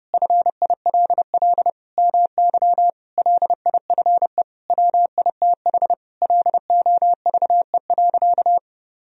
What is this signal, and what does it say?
Morse code, heard as a keyed tone, about 30 wpm, 700 Hz. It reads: FILL MY LIFE WITH LOVE.
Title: Sentences from Top 1000 Words - Review Part 3 - 30wpm